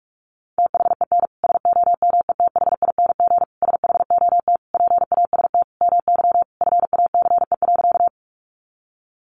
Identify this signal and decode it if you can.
Morse code, heard as a keyed tone, about 44 wpm, 700 Hz. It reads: THEN SOMETHING SHOT PAST MY FACE.
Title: Sentences from Top 1100 Words 45wpm